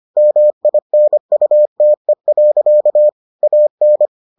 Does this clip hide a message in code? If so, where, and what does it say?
Yes: Morse code from 0.2 to 4.1 s, heard as a keyed tone, about 25 wpm, 600 Hz. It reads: MINUTE. AN